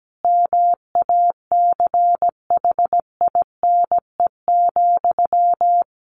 Morse code, heard as a keyed tone, about 17 wpm, 700 Hz. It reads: MACHINE,
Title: A Sound Of Thunder 17wpm